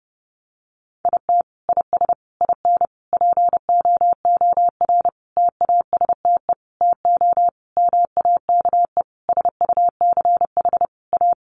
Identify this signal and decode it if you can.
Morse code, heard as a keyed tone, about 30 wpm, 700 Hz. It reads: IT IS IN POOR TASTE TO MAKE SUCH A
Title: Sentences from Top 1200 Words 30wpm